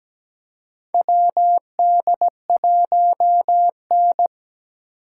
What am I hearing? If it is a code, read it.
Morse code, heard as a keyed tone, about 17 wpm, 700 Hz. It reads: WD1N
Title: USA Call Signs - 4 Characters - 4x Character Spacing - 17wpm